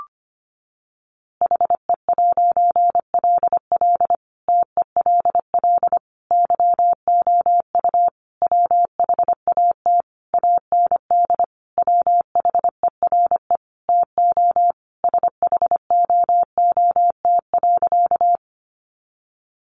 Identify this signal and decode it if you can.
Morse code, heard as a keyed tone, about 25 wpm, 700 Hz. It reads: HE'LL TELL YOU WHAT AND WHERE TO SHOOT.
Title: A Sound Of Thunder 25wpm